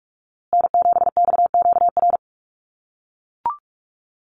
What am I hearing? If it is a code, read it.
Morse code, heard as a keyed tone, about 44 wpm, 690 Hz. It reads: N7XQR
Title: USA Call Signs Repeated 2x 45wpm